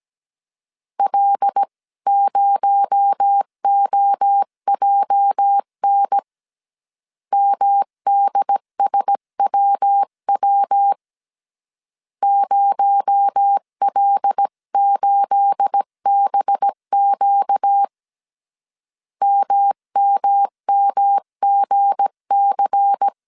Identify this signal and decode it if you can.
Morse code, heard as a keyed tone, about 17 wpm, 800 Hz. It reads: L0OJN MDSWW 0L8BQ MMMGC
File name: CW / DET / 1 9 1500C S B 16wpm 2856